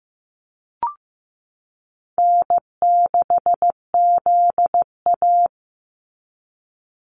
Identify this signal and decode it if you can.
Morse code, heard as a keyed tone, about 15 wpm, 700 Hz. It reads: N6ZA
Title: NA SOTA Activator Callsigns 15wpm